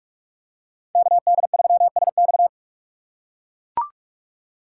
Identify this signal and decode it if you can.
Morse code, heard as a keyed tone, about 43 wpm, 700 Hz. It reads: KD3SX
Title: USA Call Signs - 5 Characters - 6x Character Spacing - 45wpm